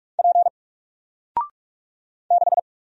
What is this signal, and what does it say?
Morse code, heard as a keyed tone, about 44 wpm, 700 Hz. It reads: P 6
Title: Single Letter-Number Alt 45wpm